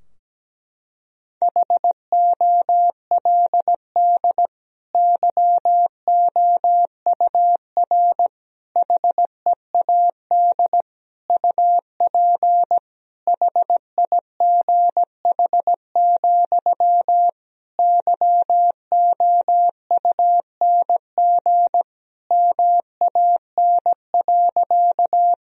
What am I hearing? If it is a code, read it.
Morse code, heard as a keyed tone, about 17 wpm, 700 Hz. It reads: HOLD YOUR HEAD UP HIGH, YOUNG MAN.
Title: Sentences from Top 1000 Words - Review Part 2 - 17wpm